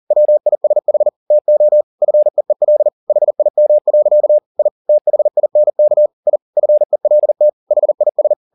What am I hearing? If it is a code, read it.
Morse code, heard as a keyed tone, about 40 wpm, 600 Hz. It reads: WISH TO FEEL HIM. I THINK I FELT HIS
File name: the old man and the sea 2 40wpm